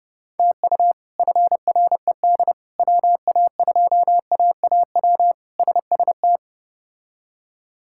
Morse code, heard as a keyed tone, about 30 wpm, 700 Hz. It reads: TU FRED WA2AAW SST
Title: K1USN SST QSO Practice - 30wpm